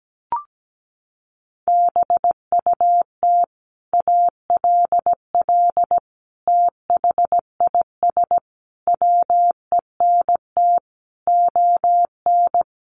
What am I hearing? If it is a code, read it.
Morse code, heard as a keyed tone, about 17 wpm, 700 Hz. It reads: BUT ALL THIS WENT ON
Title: Sentences from Top 1500 Words 17wpm